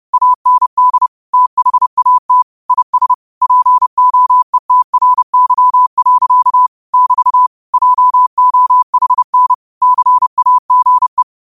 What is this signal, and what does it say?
Morse code, heard as a keyed tone, about 30 wpm, 1000 Hz. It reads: AND THAT IS POETRY. <BT> JOHN CAGE
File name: qotdfiles / 30wpm